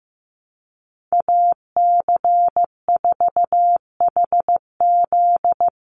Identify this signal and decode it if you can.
Morse code, heard as a keyed tone, about 15 wpm, 700 Hz. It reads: AC4HZ
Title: USA Call Signs 15wpm